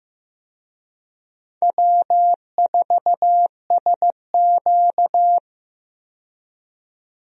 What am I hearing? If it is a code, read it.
Morse code, heard as a keyed tone, about 15 wpm, 700 Hz. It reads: W4SQ
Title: USA Call Signs - 4 Characters - 15wpm